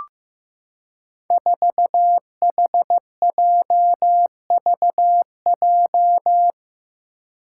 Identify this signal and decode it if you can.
Morse code, heard as a keyed tone, about 15 wpm, 700 Hz. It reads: 4HJVJ
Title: Five Challenging Letter-Number Combinations 15wpm